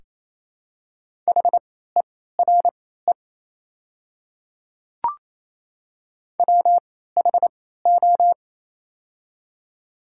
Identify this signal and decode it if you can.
Morse code, heard as a keyed tone, about 28 wpm, spaced as for 16 wpm, 700 Hz. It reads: HERE WHO
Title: 3x Character Spacing - Top 200 Words - 28wpm